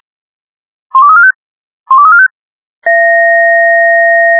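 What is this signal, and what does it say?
Five-tone selective call (ZVEI), heard as quick runs of five tones, tones of about 70 ms. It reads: 12345 12345
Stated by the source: ZVEI five-tone selective call received on SDR, code 88081.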